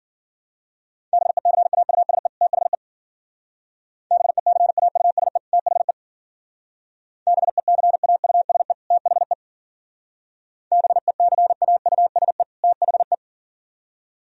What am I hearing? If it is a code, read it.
Morse code, heard as a keyed tone, about 51 wpm, 700 Hz. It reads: BECAUSE THE BECAUSE THE BECAUSE THE BECAUSE THE